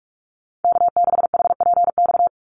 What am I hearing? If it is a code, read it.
Morse code, heard as a keyed tone, about 44 wpm, 700 Hz. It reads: K6HPX
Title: NA SOTA Chaser Callsigns 45wpm